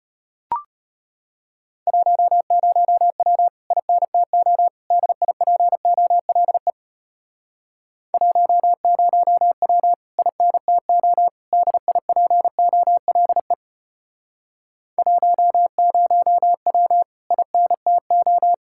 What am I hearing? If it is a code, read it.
Morse code, heard as a keyed tone, about 33 wpm, 700 Hz. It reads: 10W INTO DIPOLE 10W INTO DIPOLE 10W INTO